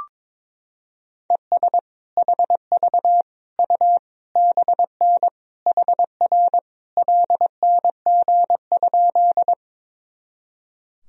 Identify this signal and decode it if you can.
Morse code, heard as a keyed tone, about 22 wpm, 700 Hz. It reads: ES HV U BN HR LNG?
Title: Sentences from Top 200 Words and Abbreviations 22wpm